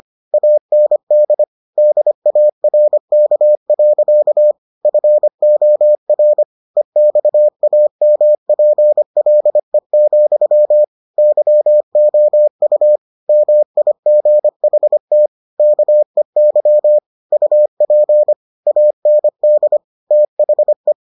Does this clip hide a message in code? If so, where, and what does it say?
Yes: Morse code from 0.3 to 20.9 s, heard as a keyed tone, about 25 wpm, 600 Hz. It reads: AND DARK. FOR EXAMPLE, YOU MIGHT KEY UP AND THE